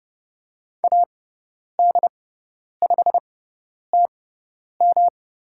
Morse code, heard as a keyed tone, about 30 wpm, spaced as for 10 wpm, 700 Hz. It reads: AD5TM
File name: Lesson 18g - Letter m - Call Signs 30wpm